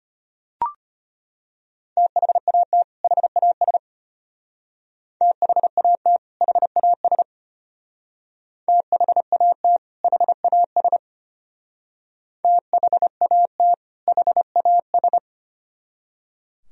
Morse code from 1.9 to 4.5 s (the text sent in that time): THAT HAS